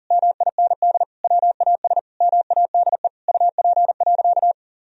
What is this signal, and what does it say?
Morse code, heard as a keyed tone, about 37 wpm, 700 Hz. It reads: MIND WAS MADE UP.
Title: Sentences from Top 1000 Words - Review Part 2 - 40wpm